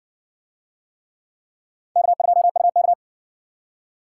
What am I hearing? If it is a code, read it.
Morse code, heard as a keyed tone, about 60 wpm, 700 Hz. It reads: K2UK